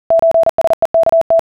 Morse code (Russian alphabet), heard as a keyed tone, about 40 wpm, 660 Hz. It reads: ЧСЕКТ